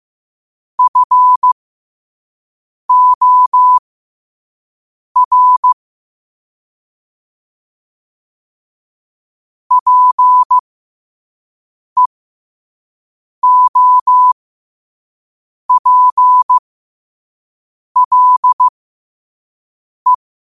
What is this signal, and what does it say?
Morse code, heard as a keyed tone, about 15 wpm, spaced as for 5 wpm, 1000 Hz. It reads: FOR PEOPLE
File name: qotdfiles / 5wpm